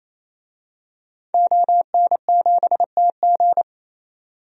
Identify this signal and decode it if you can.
Morse code, heard as a keyed tone, about 28 wpm, 700 Hz. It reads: ON7TG
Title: DXCC-100 Call Signs - 5 Characters - 28wpm